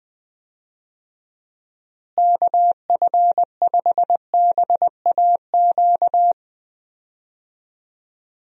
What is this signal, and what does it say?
Morse code, heard as a keyed tone, about 19 wpm, 700 Hz. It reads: KF5BAQ
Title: USA Call Signs - 6 Characters - Repeated 2x - 20wpm